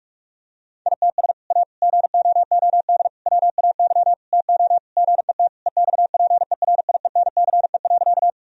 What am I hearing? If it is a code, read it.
Morse code, heard as a keyed tone, about 46 wpm, 700 Hz. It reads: ITS A GOOD WAY TO GET EXPERIENCE.
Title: Sentences from Top 1000 Words - Review Part 3 - 45wpm